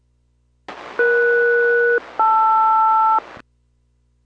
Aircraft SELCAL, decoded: ES-LR